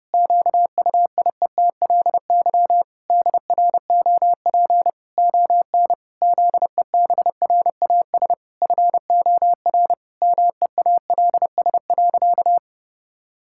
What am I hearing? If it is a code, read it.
Morse code, heard as a keyed tone, about 30 wpm, 700 Hz. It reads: QUIETLY DROP ON ZEBRAS FOR MEALS.
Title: Pangram Challenge v2 - 30wpm